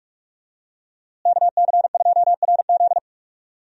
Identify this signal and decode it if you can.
Morse code, heard as a keyed tone, about 45 wpm, 700 Hz. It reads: KC2RZ
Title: USA Call Signs - 5 Characters - Repeated 2x - 45wpm